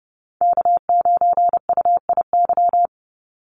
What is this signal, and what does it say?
Morse code, heard as a keyed tone, about 30 wpm, 700 Hz. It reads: K9UIY
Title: Call Signs - 30wpm